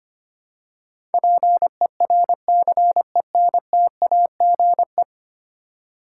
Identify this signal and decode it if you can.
Morse code, heard as a keyed tone, about 25 wpm, 700 Hz. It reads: PERCENTAGE
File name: 10 Letter Words - First 3 Letters Given - 25wpm